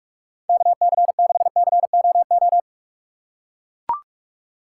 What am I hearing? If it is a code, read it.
Morse code, heard as a keyed tone, about 44 wpm, 700 Hz. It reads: KC6COO